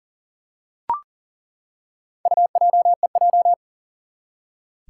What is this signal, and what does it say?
Morse code, heard as a keyed tone, about 42 wpm, 700 Hz. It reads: UJEJ
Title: Four Challenging Letter-Number Combinations 40wpm